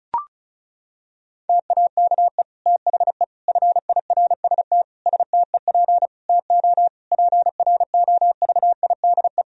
Morse code, heard as a keyed tone, about 36 wpm, 700 Hz. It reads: TAKE THE FIRST STEP TO PROVIDE
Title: Sentences from Top 1000 Words - Review Part 4 - 35wpm